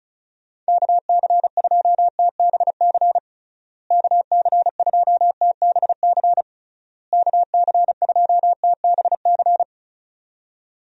Morse code, heard as a keyed tone, about 34 wpm, 700 Hz. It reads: KC2TBC KC2TBC KC2TBC
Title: USA Call Signs - 6 Characters - Repeated 3x - 35wpm